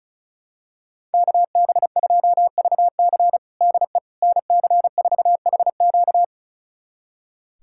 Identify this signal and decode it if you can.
Morse code, heard as a keyed tone, about 36 wpm, 700 Hz. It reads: KB2VC DE NC4HQ
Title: USA Callsign DE Callsign - 5 Characters - 35wpm